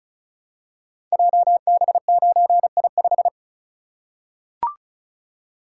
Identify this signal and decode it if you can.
Morse code, heard as a keyed tone, about 36 wpm, 700 Hz. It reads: JB9I5
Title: Code Groups - 5 Letter:Numbers - 35wpm